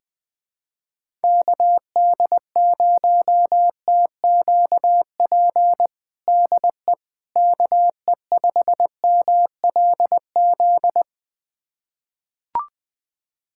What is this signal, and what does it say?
Morse code, heard as a keyed tone, about 20 wpm, 700 Hz. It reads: KD0TQP DE KE5MLZ
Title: USA Callsign DE Callsign - 6 Characters - 20wpm